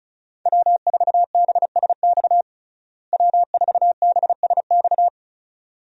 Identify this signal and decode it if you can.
Morse code, heard as a keyed tone, about 34 wpm, 700 Hz. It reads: W4BSX W4BSX
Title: USA Call Signs - 5 Characters - Repeated 2x - 35wpm